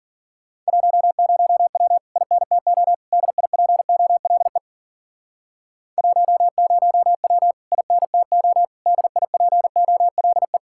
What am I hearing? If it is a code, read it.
Morse code, heard as a keyed tone, about 49 wpm, 700 Hz. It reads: 10W INTO DIPOLE 10W INTO DIPOLE
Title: Speed-Racing Common QSO Phrases - 40wpm